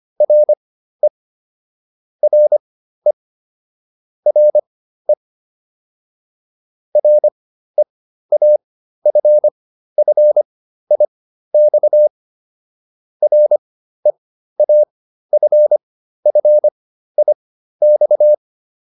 Morse code, heard as a keyed tone, about 25 wpm, spaced as for 13 wpm, 600 Hz. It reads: RE RE RE REAFFIX REAFFIX